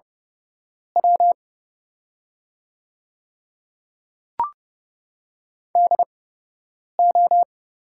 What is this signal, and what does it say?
Morse code, heard as a keyed tone, about 30 wpm, spaced as for 8 wpm, 700 Hz. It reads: W DO
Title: 8x Character Spacing - Top 300 Words - 30wpm